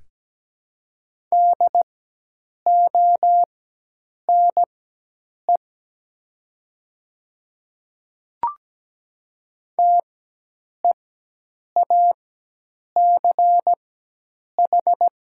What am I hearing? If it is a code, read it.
Morse code, heard as a keyed tone, about 17 wpm, spaced as for 8 wpm, 700 Hz. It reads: DONE TEACH
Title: 4x Character Spacing - Top 500 Words - 17wpm